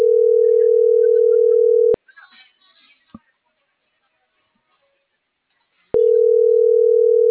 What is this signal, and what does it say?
Ringback tone (North America), 440 + 480 Hz.